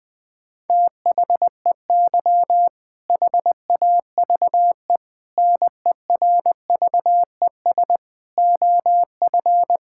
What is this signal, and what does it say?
Morse code, heard as a keyed tone, about 21 wpm, 700 Hz. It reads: THEY HAVE NERVES OF
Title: All-Age Appropriate Jokes - 20wpm